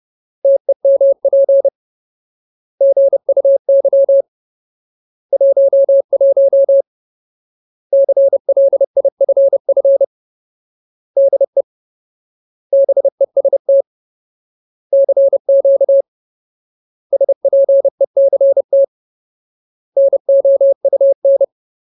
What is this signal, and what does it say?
Morse code, heard as a keyed tone, about 30 wpm, 550 Hz. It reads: TEMP GUY 11 CLIFF DE BEST CQ SPECT NOUN
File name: Ultimate Word List / 30WPM Ultimate